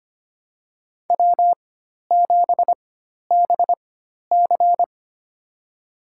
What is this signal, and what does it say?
Morse code, heard as a keyed tone, about 25 wpm, spaced as for 12 wpm, 700 Hz. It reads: W7BC
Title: USA Call Signs - 4 Characters - 4x Character Spacing - 25wpm